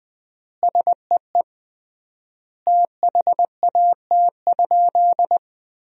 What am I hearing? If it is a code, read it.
Morse code, heard as a keyed tone, about 21 wpm, 700 Hz. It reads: SEE THAT?
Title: 3x Word-Spacing - Sentences from Top 300 Words - 20wpm